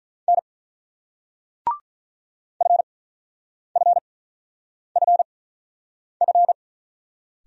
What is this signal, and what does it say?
Morse code, heard as a keyed tone, about 46 wpm, spaced as for 9 wpm, 690 Hz. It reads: N FFFF